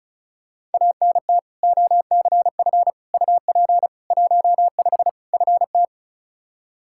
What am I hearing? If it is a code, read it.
Morse code, heard as a keyed tone, about 34 wpm, 700 Hz. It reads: ANT OCF UP 15 FT
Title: Practice QSO - Exchange 2 - Info - 35wpm